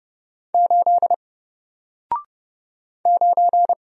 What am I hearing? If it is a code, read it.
Morse code, heard as a keyed tone, about 29 wpm, 700 Hz. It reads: 8 9
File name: Single-Letter-Number-30wpm